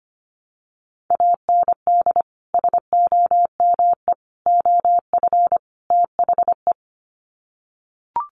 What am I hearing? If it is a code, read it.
Morse code, heard as a keyed tone, about 25 wpm, 700 Hz. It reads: AND SOME OF THE
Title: Sets of 4 Words 25wpm